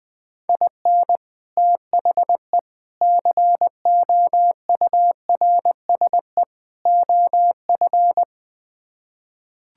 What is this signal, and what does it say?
Morse code, heard as a keyed tone, about 20 wpm, 700 Hz. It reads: IN THE COURSE OF
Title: Build A Phrase - 4-Word Phrases - 20wpm